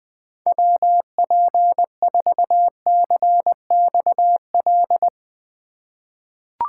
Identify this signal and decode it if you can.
Morse code, heard as a keyed tone, about 20 wpm, 700 Hz. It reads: WP4CXL
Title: USA Call Signs - 6 Characters - 4x Character Spacing - 20wpm